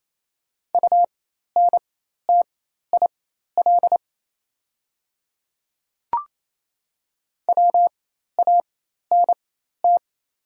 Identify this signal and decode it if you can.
Morse code, heard as a keyed tone, about 28 wpm, spaced as for 13 wpm, 690 Hz. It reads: UNTIL WANT
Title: 4x Character Spacing - Top 300 Words - 28wpm